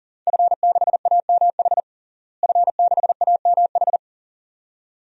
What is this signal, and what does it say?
Morse code, heard as a keyed tone, about 40 wpm, 700 Hz. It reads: F6AMH F6AMH
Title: DXCC-100 Call Signs - 5 Characters - Repeated 2x - 40wpm